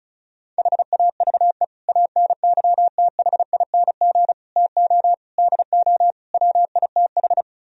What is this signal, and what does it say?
Morse code, heard as a keyed tone, about 40 wpm, 700 Hz. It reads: HAVE ANYTHING TO DO WITH